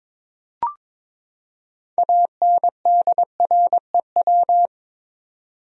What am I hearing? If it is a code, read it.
Morse code, heard as a keyed tone, about 22 wpm, 700 Hz. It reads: ANDREW
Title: USA Common Names - 22wpm